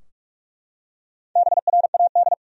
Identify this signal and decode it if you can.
Morse code, heard as a keyed tone, about 44 wpm, 700 Hz. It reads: BRAD
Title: USA Common Names - Max 4 Letters - 45wpm